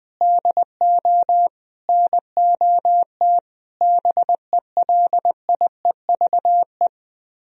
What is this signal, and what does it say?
Morse code, heard as a keyed tone, about 20 wpm, 700 Hz. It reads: DO NOT BELIEVE